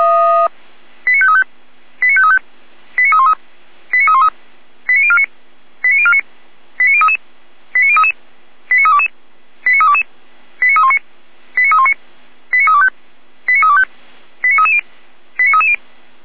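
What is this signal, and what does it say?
Five-tone selective call (ZVEI), heard as quick runs of five tones, tones of about 70 ms. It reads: 89535 89535 89313 89313 89049 89049 89033 89033 89322 89322 89319 89319 89325 89325 89330 89330
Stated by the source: ZVEI five-tone call received by SDR.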